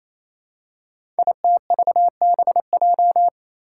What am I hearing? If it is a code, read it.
Morse code, heard as a keyed tone, about 27 wpm, 700 Hz. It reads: ITVBJ
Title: Code Groups - 5 Letters - 28wpm